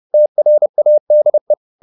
Morse code, heard as a keyed tone, about 30 wpm, 600 Hz. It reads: TRADE